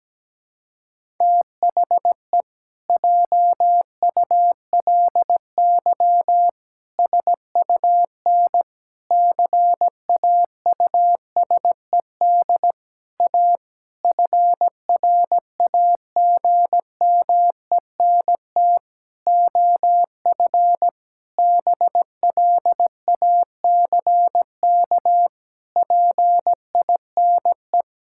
Morse code, heard as a keyed tone, about 17 wpm, 700 Hz. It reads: THE JULY SUN CAUSED A FRAGMENT OF BLACK PINE
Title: Pangram Challenge v2 - 17wpm